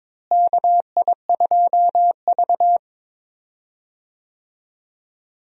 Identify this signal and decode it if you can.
Morse code, heard as a keyed tone, about 22 wpm, 700 Hz. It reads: KI2V
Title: USA Call Signs - 4 Characters - Repeated 3x - 22wpm